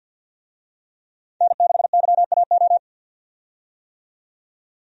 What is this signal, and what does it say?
Morse code, heard as a keyed tone, about 48 wpm, 700 Hz. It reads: N6YAO
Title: USA Call Signs - 5 Characters - 50wpm